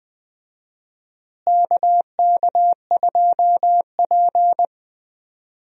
Morse code, heard as a keyed tone, about 19 wpm, 700 Hz. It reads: KK2P
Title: USA Call Signs - 4 Characters - Repeated 3x - 20wpm